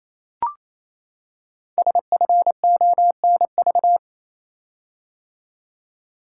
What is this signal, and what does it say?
Morse code, heard as a keyed tone, about 29 wpm, 700 Hz. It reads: SFONV
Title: Code Groups - 5 Letters - 28wpm